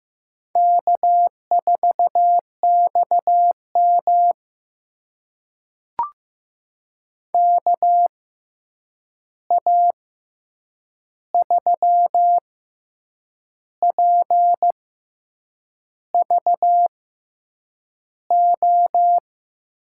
Morse code, heard as a keyed tone, about 15 wpm, spaced as for 7 wpm, 700 Hz. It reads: K4XM KA3PVO
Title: USA Call Signs - 6x Character Spacing - 15wpm